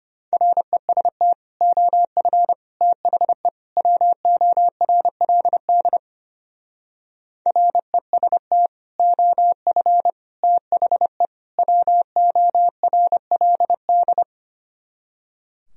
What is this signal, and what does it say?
Morse code, heard as a keyed tone, about 31 wpm, 700 Hz. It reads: REST OF THE WORLD REST OF THE WORLD
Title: Speed-Racing - Sets of 4 Words 25wpm